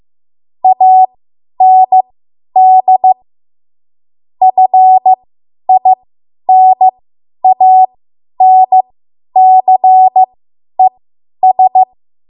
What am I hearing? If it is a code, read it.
Morse code, heard as a keyed tone, about 15 wpm, spaced as for 10 wpm, 750 Hz. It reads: AND FINANCES